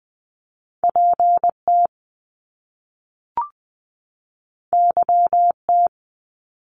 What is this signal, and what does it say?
Morse code, heard as a keyed tone, about 19 wpm, 700 Hz. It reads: PT YT
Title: Top Two Letter Combinations in Words 20wpm